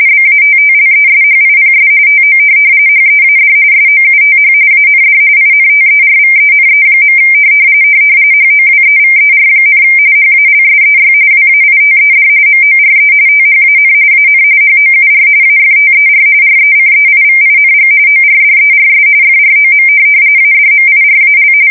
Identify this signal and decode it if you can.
RTTY, about 45.45 baud. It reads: BATTLESHIPS TO HEAD SOUTH AND INTERVENE. HOWEVER, HALSEY WAITED MORE THAN AN HOUR AFTER RECEIVING ORDERS FROM ADMIRAL